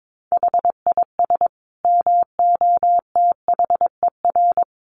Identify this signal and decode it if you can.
Morse code, heard as a keyed tone, about 22 wpm, 700 Hz. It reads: HIS MOTHER
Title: Sentences from Top 1200 Words 22wpm